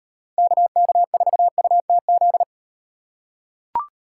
Morse code, heard as a keyed tone, about 38 wpm, 700 Hz. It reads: KK4UTZ